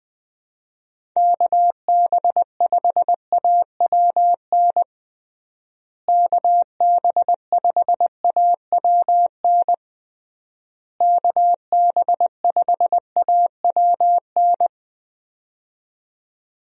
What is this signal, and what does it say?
Morse code, heard as a keyed tone, about 21 wpm, 700 Hz. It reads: KB5AWN KB5AWN KB5AWN
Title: USA Call Signs - 6 Characters - Repeated 3x - 20wpm